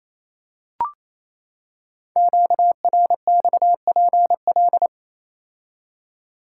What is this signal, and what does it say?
Morse code, heard as a keyed tone, about 27 wpm, 700 Hz. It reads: QRXPL
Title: Code Groups - 5 Letters - 28wpm